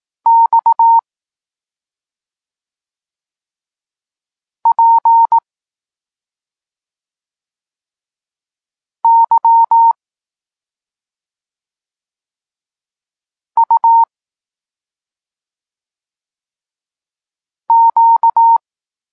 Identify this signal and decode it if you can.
Morse code, heard as a keyed tone, about 18 wpm, 920 Hz. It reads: X P Y U Q
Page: Tekst 32 – Doskonalenie odbioru poznanych liter